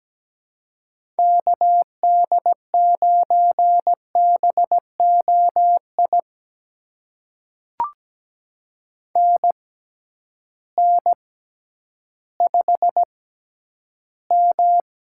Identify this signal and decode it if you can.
Morse code, heard as a keyed tone, about 17 wpm, spaced as for 10 wpm, 700 Hz. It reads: KD9BOI NN5M
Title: USA Call Signs - 6x Character Spacing - 17wpm